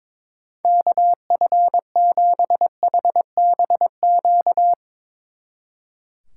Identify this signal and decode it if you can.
Morse code, heard as a keyed tone, about 24 wpm, 700 Hz. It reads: KF7HBQ
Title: Speed Racing - USA 2x3 Call Signs 22wpm